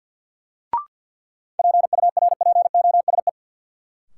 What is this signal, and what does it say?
Morse code, heard as a keyed tone, about 53 wpm, 700 Hz. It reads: PURPOSE
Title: Sentences from Top 800 Words - New Word Drill - 50wpm